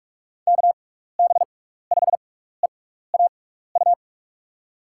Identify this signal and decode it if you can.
Morse code, heard as a keyed tone, about 44 wpm, spaced as for 15 wpm, 700 Hz. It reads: KB5EAU
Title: USA Call Signs - 6 Characters - 6x Character Spacing - 45wpm